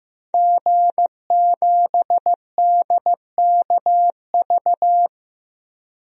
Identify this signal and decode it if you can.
Morse code, heard as a keyed tone, about 16 wpm, 700 Hz. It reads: G7DKV
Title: DXCC-100 Call Signs - 5 Characters - 15wpm